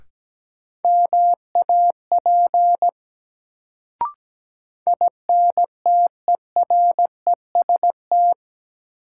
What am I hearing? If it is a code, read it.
Morse code, heard as a keyed tone, about 17 wpm, 700 Hz. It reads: MAP INTEREST
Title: Sentences from Top 400 Words - New Word Drill - 17wpm